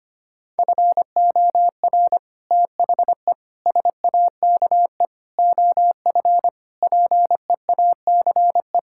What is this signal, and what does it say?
Morse code, heard as a keyed tone, about 25 wpm, 700 Hz. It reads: FOR THE SAKE OF PEACE
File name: practice / 25wpm / Sentences with 4-Word Phrases - 25wpm